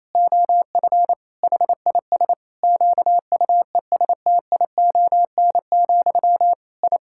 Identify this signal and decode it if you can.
Morse code, heard as a keyed tone, about 28 wpm, 700 Hz. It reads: OF HIS QUESTION, I
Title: Sentences from Top 1000 Words - Review Part 6 - 28wpm